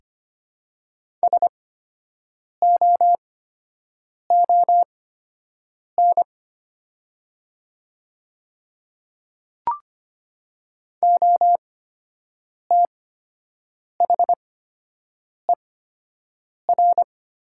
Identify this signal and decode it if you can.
Morse code, heard as a keyed tone, about 25 wpm, spaced as for 7 wpm, 700 Hz. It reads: SOON OTHER
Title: 8x Character Spacing - Top 500 Words - 25wpm